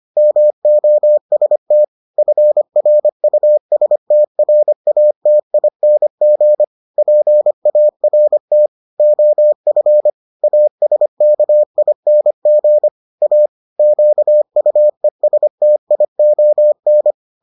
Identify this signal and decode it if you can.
Morse code, heard as a keyed tone, about 25 wpm, 600 Hz. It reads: MOST FRUSTRATING PART OF ASKING A QUESTION